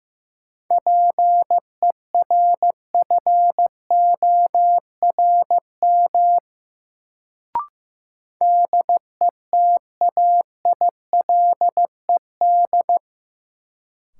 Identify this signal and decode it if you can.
Morse code, heard as a keyed tone, about 15 wpm, 700 Hz. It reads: PERFORM DETAILED
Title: Sentences from Top 1500 Words - New Word Drill Alt - 15wpm